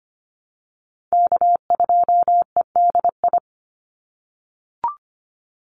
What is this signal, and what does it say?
Morse code, heard as a keyed tone, about 25 wpm, 700 Hz. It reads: K2EDI
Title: USA Call Signs 25wpm